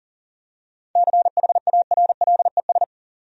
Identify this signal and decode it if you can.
Morse code, heard as a keyed tone, about 40 wpm, 700 Hz. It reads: CHARLES
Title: USA Common Names - 40wpm